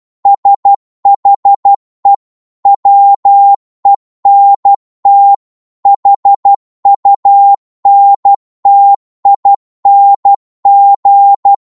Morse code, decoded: SHE WENT HUNTING